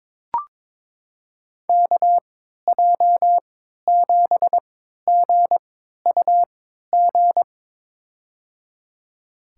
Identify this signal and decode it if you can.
Morse code, heard as a keyed tone, about 22 wpm, spaced as for 13 wpm, 700 Hz. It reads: KJ7GUG
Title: USA Call Signs - 6 Characters - 3x Character Spacing - 22wpm